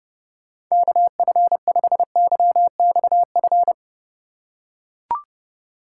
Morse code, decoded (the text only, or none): KF5YXF